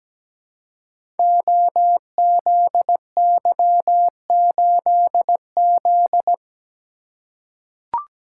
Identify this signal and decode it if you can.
Morse code, heard as a keyed tone, about 17 wpm, 700 Hz. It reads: OZY8Z
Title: Code Groups - 5 Letter:Numbers - 17wpm